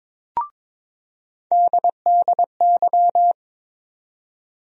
Morse code, heard as a keyed tone, about 22 wpm, 700 Hz. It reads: DDY
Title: USA Call Sign Postfixes - 3 Letters 22wpm